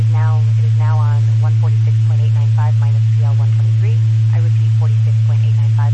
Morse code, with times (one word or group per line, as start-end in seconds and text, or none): none